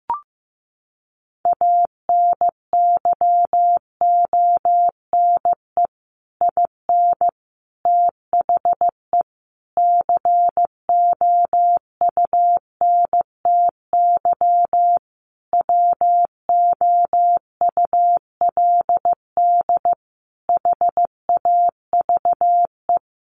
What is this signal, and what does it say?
Morse code, heard as a keyed tone, about 15 wpm, 700 Hz. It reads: ANYONE IN THE COUNTY WOULD HAVE
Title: Sentences from Top 1200 Words 15wpm